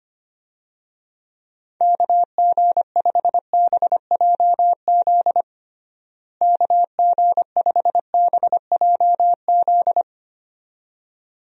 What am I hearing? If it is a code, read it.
Morse code, heard as a keyed tone, about 25 wpm, 700 Hz. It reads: KG5BJZ KG5BJZ
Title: USA Call Signs - 6 Characters - Repeated 2x - 25wpm